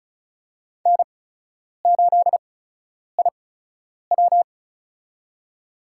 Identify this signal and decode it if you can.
Morse code, heard as a keyed tone, about 36 wpm, spaced as for 10 wpm, 700 Hz. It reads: N8IW
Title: USA Call Signs - 8x Character Spacing - 35wpm